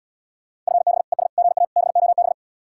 Morse code, heard as a keyed tone, about 77 wpm, 700 Hz. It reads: FB ES GA BOB